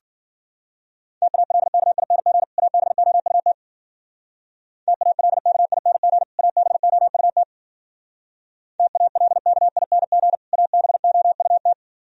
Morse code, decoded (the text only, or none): TALKING ABOUT TALKING ABOUT TALKING ABOUT